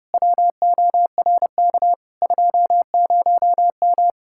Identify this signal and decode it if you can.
Morse code, heard as a keyed tone, about 36 wpm, 700 Hz. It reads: WORK 20M
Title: Speed-Racing Common QSO Phrases - 30wpm